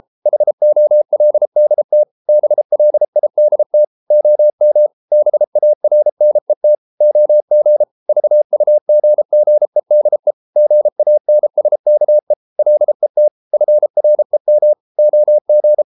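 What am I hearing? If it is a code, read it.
Morse code, heard as a keyed tone, about 33 wpm, 600 Hz. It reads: HOLDT BLIDT OM BARNET OG VUGGEDE GANSKE LET FREM OG
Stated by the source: Morse code at 33 wpm, QRQ.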